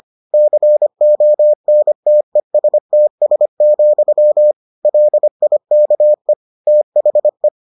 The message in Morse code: CONTESTS, LIKE THE

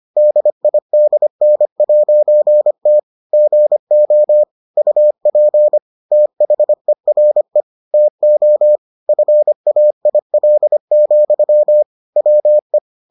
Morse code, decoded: DIDN'T GO UP THERE TO FAIL, WE

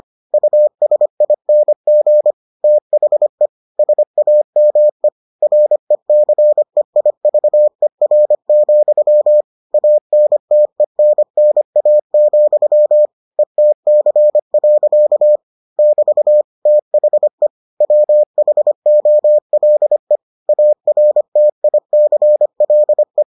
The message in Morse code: USING THE SAME RECEIVER, ANTENNA, ETC. <BT> THE WHOLE ARTICLE

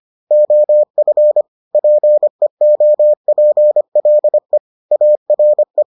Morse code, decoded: OF PEOPLE ARE

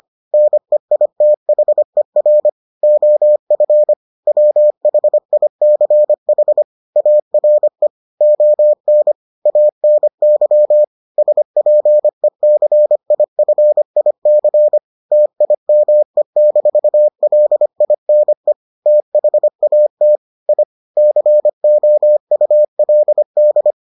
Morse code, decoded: NEITHER OF WHICH ARE ON ANY SPECIFIC TIME-LINE THAT I COULD

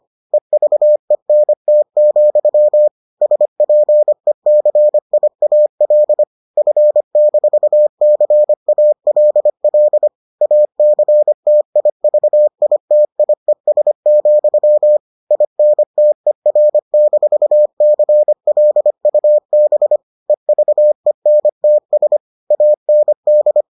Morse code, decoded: EVENT, SPECIAL F-CALL ACTIVITIES, INTER-CLUB EVENTS AND

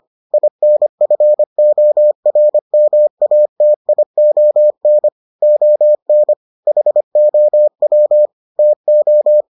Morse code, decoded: INFORMATION ON HOW TO

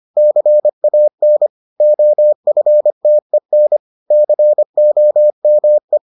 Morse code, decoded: CAN OFTEN COME

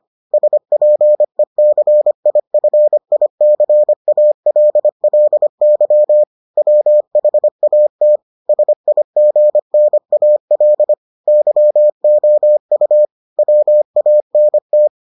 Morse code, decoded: SPECIFICALLY WHAT SIGNAL YOU WANT

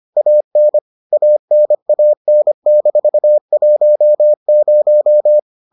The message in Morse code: AN ANAN-10

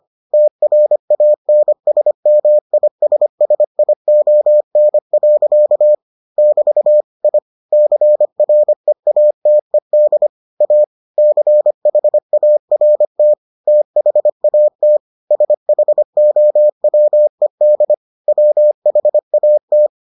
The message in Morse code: TRANSMISSION. <BT> I CREATED A CHART THAT SHOWED WHAT

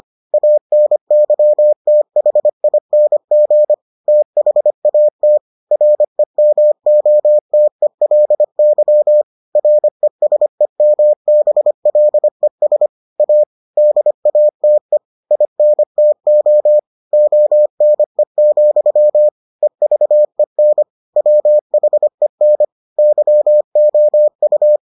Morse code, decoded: ANYTHING THAT REMOTELY RESEMBLES A DATE INTO ONE, EVEN WHEN YOU

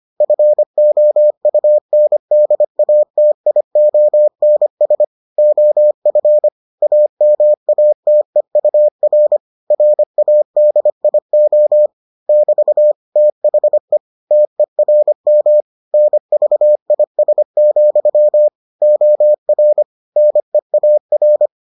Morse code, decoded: FOUNDATIONS OF AMATEUR RADIO <BT> THE TERM NVIS, OR NEAR

